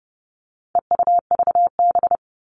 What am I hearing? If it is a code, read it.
Morse code, heard as a keyed tone, about 30 wpm, 700 Hz. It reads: EUVB
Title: All Cut Numbers - 4-Digit 30wpm